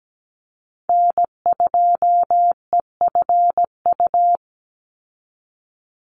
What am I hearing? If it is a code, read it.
Morse code, heard as a keyed tone, about 17 wpm, 700 Hz. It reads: N2EFU
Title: USA Call Signs 17wpm